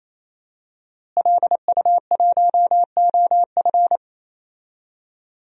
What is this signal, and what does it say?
Morse code, heard as a keyed tone, about 28 wpm, 700 Hz. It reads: LU1OF
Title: DXCC-100 Call Signs - 5 Characters - Repeated 3x - 28wpm